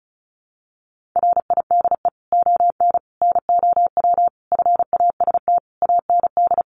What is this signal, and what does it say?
Morse code, heard as a keyed tone, about 34 wpm, 700 Hz. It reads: RIDE ON NOW FAST AND
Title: Sentences from Top 1500 Words 35wpm